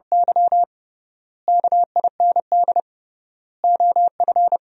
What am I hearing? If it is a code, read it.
Morse code, heard as a keyed tone, about 32 wpm, 700 Hz. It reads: Y KIND OF
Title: 3x Word-Spacing - Sentences from Top 200 Words - 30wpm